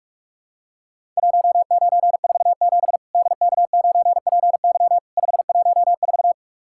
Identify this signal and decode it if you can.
Morse code, heard as a keyed tone, about 44 wpm, 700 Hz. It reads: 1947 DK9PY 514
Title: Worked All Europe DX Contest - QTC Exchange - 45wpm